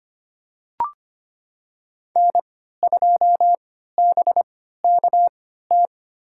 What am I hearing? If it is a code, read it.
Morse code, heard as a keyed tone, about 25 wpm, spaced as for 14 wpm, 700 Hz. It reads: N2BKT
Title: USA Call Signs - 5 Characters - 3x Character Spacing - 25wpm